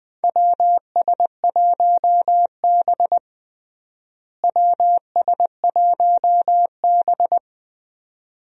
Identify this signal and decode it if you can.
Morse code, heard as a keyed tone, about 20 wpm, 700 Hz. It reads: WS1B WS1B
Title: USA Call Signs - 4 Characters - Repeated 2x - 20wpm